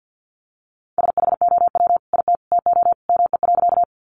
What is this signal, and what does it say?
Morse code, heard as a keyed tone, about 50 wpm, 700 Hz. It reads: SHOW IT TO ME.
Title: Sentences from Top 200 Words 50wpm